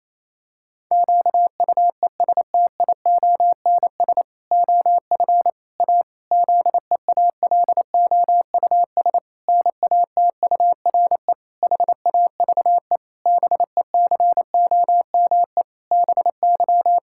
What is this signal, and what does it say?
Morse code, heard as a keyed tone, about 28 wpm, 700 Hz. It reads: QUESTIONS OF A ZEALOUS NATURE HAVE BECOME BY
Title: Pangram Challenge v2 - 28wpm